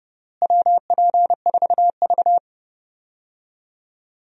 Morse code, heard as a keyed tone, about 29 wpm, 700 Hz. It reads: WP4V
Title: USA Call Signs - 4 Characters - Repeated 3x - 30wpm